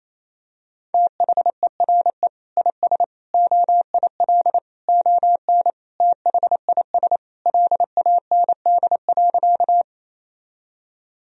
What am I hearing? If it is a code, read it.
Morse code, heard as a keyed tone, about 28 wpm, 700 Hz. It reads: THERE IS OIL ON THIS LAND.
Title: Sentences from Top 1000 Words - Review Part 4 - 28wpm